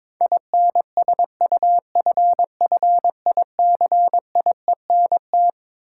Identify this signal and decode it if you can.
Morse code, heard as a keyed tone, about 22 wpm, 700 Hz. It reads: INSUFFICIENT
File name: Twelve Letter Words - 22wpm